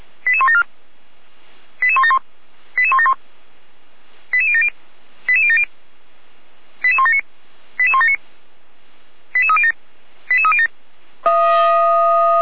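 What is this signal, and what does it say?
Five-tone selective call (ZVEI), heard as quick runs of five tones, tones of about 70 ms. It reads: 88174 88171 88171 88070 88070 88179 88179 88307 88307
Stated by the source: ZVEI five-tone selective call received on SDR, address 88090.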